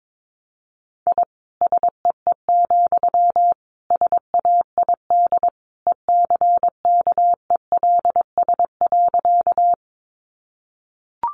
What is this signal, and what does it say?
Morse code, heard as a keyed tone, about 22 wpm, 700 Hz. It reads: I SEE, SAID ECKELS.
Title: A Sound Of Thunder 22wpm